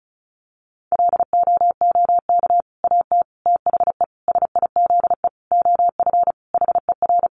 Morse code, heard as a keyed tone, about 34 wpm, 700 Hz. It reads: LOOK AT THE SIZE OF HER
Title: Sentences from Top 1200 Words 35wpm